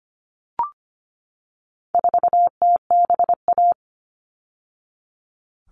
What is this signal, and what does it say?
Morse code, heard as a keyed tone, about 25 wpm, 700 Hz. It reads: 4TBA
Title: All Cut Numbers - 4-Digit 25wpm